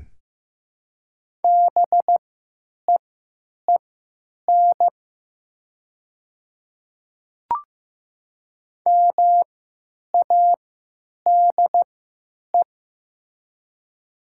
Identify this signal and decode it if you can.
Morse code, heard as a keyed tone, about 15 wpm, spaced as for 9 wpm, 700 Hz. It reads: BEEN MADE
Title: 3x Character Spacing - Top 300 Words - 15wpm